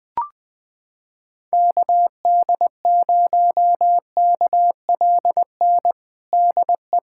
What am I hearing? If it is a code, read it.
Morse code, heard as a keyed tone, about 20 wpm, 700 Hz. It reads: KD0KLN DE
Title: USA Callsign DE Callsign - 6 Characters - 20wpm